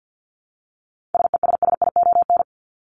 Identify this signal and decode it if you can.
Morse code, heard as a keyed tone, about 50 wpm, 700 Hz. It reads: SESSION
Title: Top 1501-1750 Words 50wpm